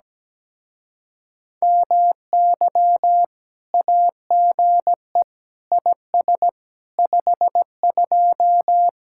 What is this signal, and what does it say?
Morse code, heard as a keyed tone, about 17 wpm, 700 Hz. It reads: MY AGE IS 52
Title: Speed-Racing Common QSO Phrases Encore - 17wpm